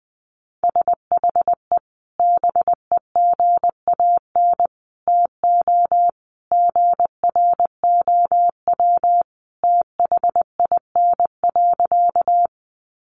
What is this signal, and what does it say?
Morse code, heard as a keyed tone, about 20 wpm, 700 Hz. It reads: SHE BEGAN TO GROW THIN.
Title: Sentences from Top 1200 Words 20wpm